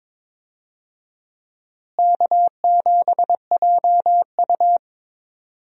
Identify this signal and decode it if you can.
Morse code, heard as a keyed tone, about 22 wpm, 700 Hz. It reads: K7JU
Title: USA Call Signs - 4 Characters - Repeated 2x - 22wpm